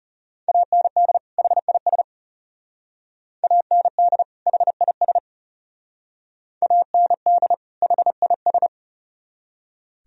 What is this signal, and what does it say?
Morse code, heard as a keyed tone, about 35 wpm, 700 Hz. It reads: AND HIS AND HIS AND HIS
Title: Speed-Racing - Sets of 2 Words 30wpm